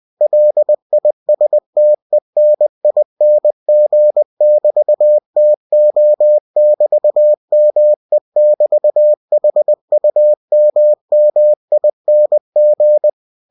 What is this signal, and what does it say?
Morse code, heard as a keyed tone, about 20 wpm, 600 Hz. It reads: LISTENING<BT>TO<BT>ME<BT>HUMMING
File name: cw-books / winnie-the-pooh / winnie-20wpm-page-0013